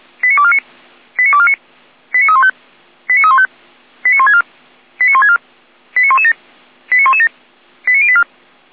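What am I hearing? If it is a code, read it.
Five-tone selective call (ZVEI), heard as quick runs of five tones, tones of about 70 ms. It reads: 89249 89249 89315 89315 89164 89164 89117 89117 89084
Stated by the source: ZVEI five-tone selective call 89280 received by SDR.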